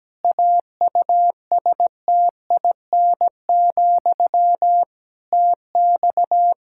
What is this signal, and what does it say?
Morse code, heard as a keyed tone, about 17 wpm, 700 Hz. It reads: AUSTIN, TX
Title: Practice QSO - Exchange 1 - Beginning - 17wpm